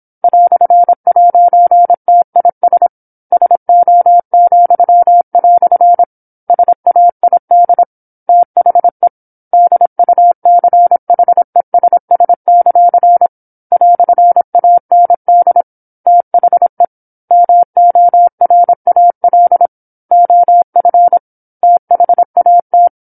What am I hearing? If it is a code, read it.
Morse code, heard as a keyed tone, about 27 wpm, 700 Hz. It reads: "'TIS SO," SAID THE DUCHESS; "AND THE MORAL OF THAT